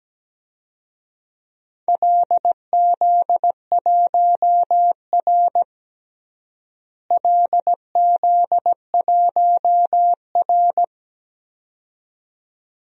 Morse code, heard as a keyed tone, about 17 wpm, 700 Hz. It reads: LZ1R LZ1R
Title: DXCC-100 Call Signs - 4 Characters - Repeated 2x - 17wpm